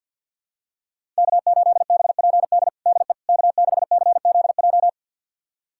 Morse code, heard as a keyed tone, about 46 wpm, 700 Hz. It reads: K8BPD DE K6CZJ
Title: USA Callsign DE Callsign - 5 Characters - 50wpm